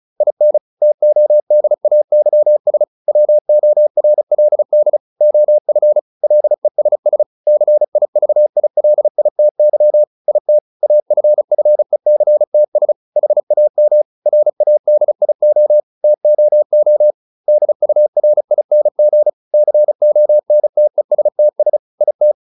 Morse code, heard as a keyed tone, about 34 wpm, 600 Hz. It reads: IN TODAYS WORLD OF LESS CIVILITY IT AFFECTS HAM RADIO TOO DURING CONTESTS IT
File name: SS309 35